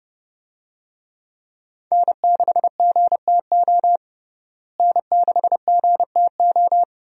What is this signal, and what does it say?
Morse code, heard as a keyed tone, about 32 wpm, 700 Hz. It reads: N6GTO N6GTO
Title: USA Call Signs - 5 Characters - Repeated 3x - 30wpm